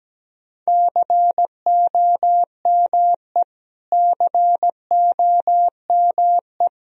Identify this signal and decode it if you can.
Morse code, heard as a keyed tone, about 17 wpm, 700 Hz. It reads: COME COME
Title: Top 100 Words Repeated 17wpm